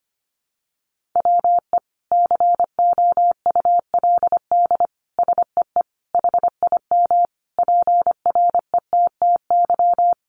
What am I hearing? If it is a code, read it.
Morse code, heard as a keyed tone, about 25 wpm, 700 Hz. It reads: WE COULD SEE HIM PRETTY
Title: Sentences from Top 1000 Words 25wpm